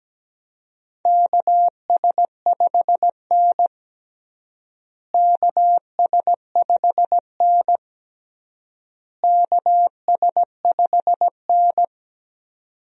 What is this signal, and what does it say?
Morse code, heard as a keyed tone, about 17 wpm, 700 Hz. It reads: KS5N KS5N KS5N
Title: USA Call Signs - 4 Characters - Repeated 3x - 17wpm